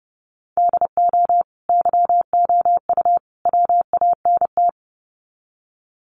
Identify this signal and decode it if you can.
Morse code, heard as a keyed tone, about 30 wpm, 700 Hz. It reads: DO YOU WANT
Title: Sets of 3 Words 30wpm